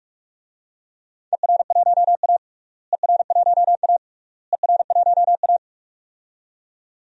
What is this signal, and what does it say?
Morse code, heard as a keyed tone, about 46 wpm, 700 Hz. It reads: ER1A ER1A ER1A
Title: DXCC-100 Call Signs - 4 Characters - Repeated 3x - 45wpm